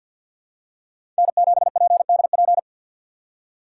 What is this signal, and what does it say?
Morse code, heard as a keyed tone, about 46 wpm, 690 Hz. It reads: N7PDP
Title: USA Call Signs - 2x Character Spacing - 50wpm